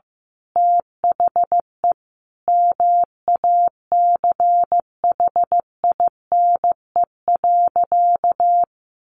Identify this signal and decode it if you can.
Morse code, heard as a keyed tone, about 15 wpm, 700 Hz. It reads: THE MACHINE.
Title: A Sound Of Thunder 15wpm